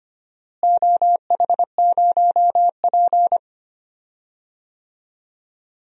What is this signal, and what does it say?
Morse code, heard as a keyed tone, about 25 wpm, 700 Hz. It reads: OH0P
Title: DXCC-100 Call Signs - 4 Characters - Repeated 2x - 25wpm